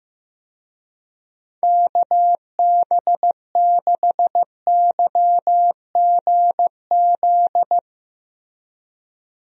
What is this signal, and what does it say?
Morse code, heard as a keyed tone, about 16 wpm, 700 Hz. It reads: KB6YGZ
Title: USA Call Signs - 6 Characters - Repeated 2x - 15wpm